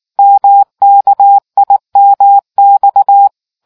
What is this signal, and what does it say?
Morse code, heard as a keyed tone, about 19 wpm, 800 Hz. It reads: MKIMX